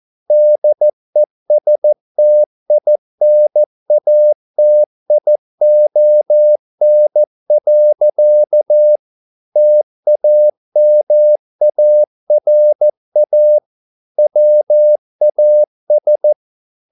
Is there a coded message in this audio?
Yes: Morse code from 0.3 to 16.3 s, heard as a keyed tone, about 14 wpm, 600 Hz. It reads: DESTINATION. TAMARA WAS